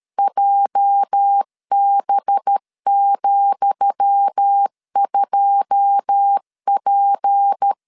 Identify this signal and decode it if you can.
Morse code, heard as a keyed tone, about 13 wpm, 800 Hz. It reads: JB,2P
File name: LEZIONI / 10 SIGNS / 10D COMA 12wpm 2544